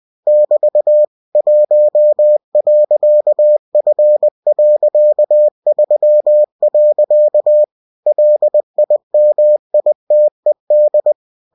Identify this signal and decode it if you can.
Morse code, heard as a keyed tone, about 20 wpm, 600 Hz. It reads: <BT> 1.F.3. LIMITED
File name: cw-books / winnie-the-pooh / winnie-20wpm-page-0119